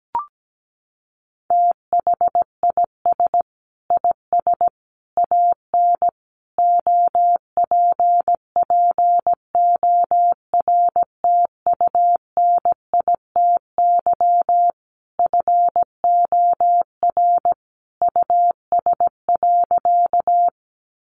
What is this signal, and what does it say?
Morse code, heard as a keyed tone, about 17 wpm, 700 Hz. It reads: THIS IS AN OPPORTUNITY FOR US.
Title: Sentences from Top 1000 Words 17wpm